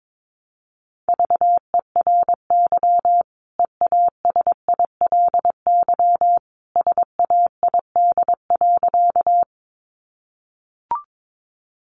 Morse code, heard as a keyed tone, about 22 wpm, 700 Hz. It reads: VERY EASILY SAID.
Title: Sentences from Top 900 Words 22wpm